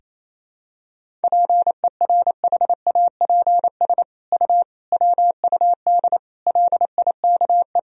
Morse code, heard as a keyed tone, about 28 wpm, 700 Hz. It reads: PERHAPS U WUD LIKE
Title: Sentences from Top 500 Words and Abbreviations 28wpm